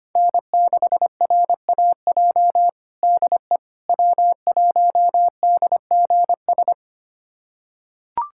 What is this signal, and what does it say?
Morse code, heard as a keyed tone, about 25 wpm, 700 Hz. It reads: N6RAJ DE W1DGS
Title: USA Callsign DE Callsign - 5 Characters - 25wpm